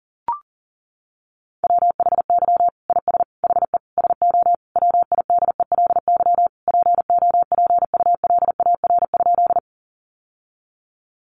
Morse code, heard as a keyed tone, about 37 wpm, 700 Hz. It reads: WHY IS HE SO WIDELY POPULAR?
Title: Sentences from Top 1200 Words 40wpm